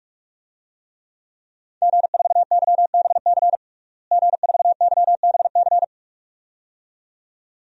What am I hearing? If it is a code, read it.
Morse code, heard as a keyed tone, about 46 wpm, 700 Hz. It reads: G4YBC G4YBC
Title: DXCC-100 Call Signs - 5 Characters - Repeated 2x - 45wpm